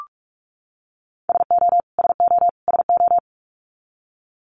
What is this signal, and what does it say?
Morse code, heard as a keyed tone, about 44 wpm, 700 Hz. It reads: SO SO SO
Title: Top 500 Words Repeated 45wpm